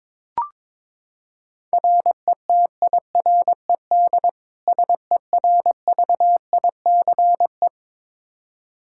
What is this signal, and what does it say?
Morse code, heard as a keyed tone, about 22 wpm, 700 Hz. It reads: RETIRED SERVICE
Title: K1USN SST QSO Practice 22wpm